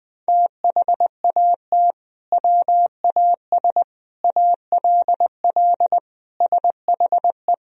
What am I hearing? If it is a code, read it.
Morse code, heard as a keyed tone, about 20 wpm, 700 Hz. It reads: THAT WAS ALL SHE
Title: Sentences from Top 100 Words and Abbreviations 20wpm